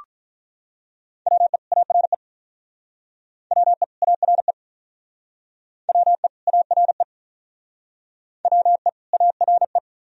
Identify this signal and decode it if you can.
Morse code, heard as a keyed tone, about 46 wpm, 700 Hz. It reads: WE ARE WE ARE WE ARE WE ARE